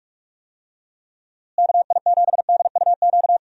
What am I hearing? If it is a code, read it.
Morse code, heard as a keyed tone, about 41 wpm, 700 Hz. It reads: KI7DUQ
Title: USA Call Signs - 6 Characters - Repeated 3x - 45wpm